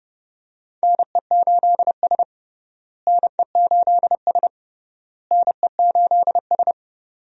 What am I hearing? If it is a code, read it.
Morse code, heard as a keyed tone, about 32 wpm, 700 Hz. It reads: NE8S NE8S NE8S
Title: USA Call Signs - 4 Characters - Repeated 3x - 30wpm